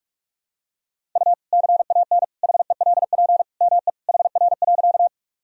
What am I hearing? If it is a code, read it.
Morse code, heard as a keyed tone, about 46 wpm, 700 Hz. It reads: U CAN HELP ME HR.
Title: Sentences from Top 200 Words and Abbreviations 45wpm